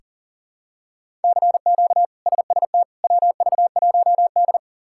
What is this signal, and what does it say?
Morse code, heard as a keyed tone, about 40 wpm, 700 Hz. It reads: CQ SST WV1D
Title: K1USN SST QSO Practice - 40wpm